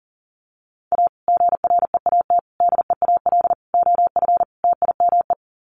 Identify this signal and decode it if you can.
Morse code, heard as a keyed tone, about 40 wpm, 700 Hz. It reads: A GREAT DEAL OF TIME
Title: Sets of 5 Words Encore2 40wpm